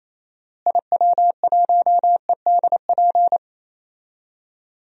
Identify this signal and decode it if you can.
Morse code, heard as a keyed tone, about 28 wpm, 700 Hz. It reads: IW1EDP
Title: DXCC-100 Call Signs - 6 Characters - Repeated 3x - 28wpm